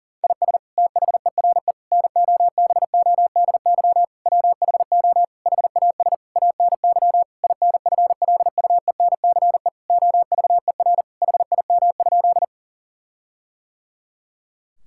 Morse code, heard as a keyed tone, about 40 wpm, 700 Hz. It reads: IS THERE NOBODY WHO HAS ANY INFLUENCE OVER HIM?
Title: Sentences from Top 1600 Words 40wpm